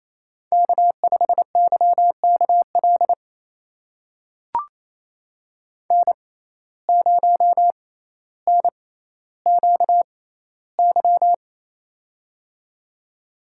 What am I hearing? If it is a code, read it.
Morse code, heard as a keyed tone, about 28 wpm, spaced as for 14 wpm, 700 Hz. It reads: K5YKL N0NQY
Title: USA Call Signs - 5 Characters - 6x Character Spacing - 28wpm